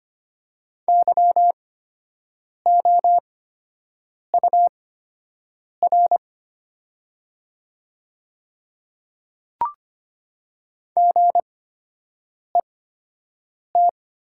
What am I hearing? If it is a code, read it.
Morse code, heard as a keyed tone, about 25 wpm, spaced as for 7 wpm, 690 Hz. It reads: YOUR GET
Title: 8x Character Spacing - Top 200 Words - 25wpm